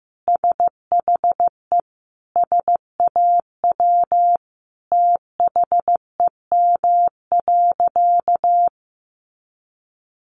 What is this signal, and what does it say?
Morse code, heard as a keyed tone, about 15 wpm, 700 Hz. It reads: SHE SAW THEM.
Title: Sentences from Top 1300 Words 15wpm